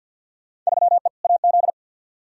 Morse code, heard as a keyed tone, about 51 wpm, 700 Hz. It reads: 3E AZ
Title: Field Day Exchanges Encore - 50wpm